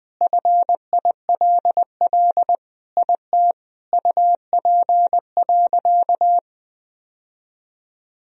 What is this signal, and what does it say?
Morse code, heard as a keyed tone, about 20 wpm, 700 Hz. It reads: FILL IT UP.
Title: Sentences from Top 1000 Words - Review Part 3 - 20wpm